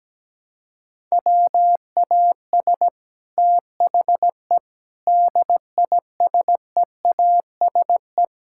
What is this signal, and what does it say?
Morse code, heard as a keyed tone, about 17 wpm, 700 Hz. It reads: WAS THE DISEASE
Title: Sentences from Top 1000 Words - Review Part 6 - 17wpm